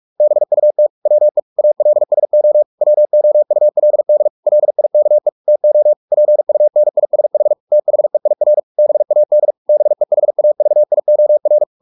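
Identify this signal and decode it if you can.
Morse code, heard as a keyed tone, about 44 wpm, 600 Hz. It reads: BUT WE ALSO WOULD LIKE TO PUNISH THEIR BAD BEHAVIOR
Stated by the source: Morse practice at 45 wpm, the novel Never by Ken Follett in English, from CW med Gnister.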